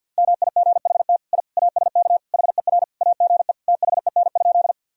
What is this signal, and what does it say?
Morse code, heard as a keyed tone, about 49 wpm, 700 Hz. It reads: MIGHT I ASK HER AGE THEN?
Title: Sentences from Top 1000 Words - Review Part 3 - 50wpm